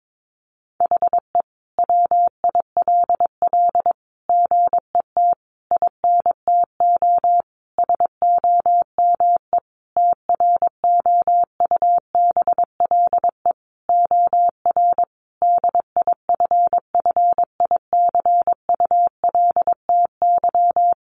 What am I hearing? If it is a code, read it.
Morse code, heard as a keyed tone, about 22 wpm, 700 Hz. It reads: HE WILL GET INTO SOME TROUBLE OR DIFFICULTY
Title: Sentences from Top 1500 Words 22wpm